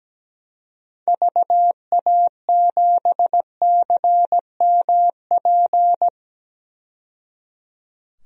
Morse code, heard as a keyed tone, about 17 wpm, 700 Hz. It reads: VA7CMP
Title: DXCC-100 Call Signs - 6 Characters - 17wpm